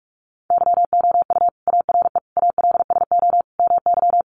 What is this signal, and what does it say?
Morse code, heard as a keyed tone, about 46 wpm, 700 Hz. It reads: YOU ARE ALSO MY
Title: Sentences from Top 1300 Words 45wpm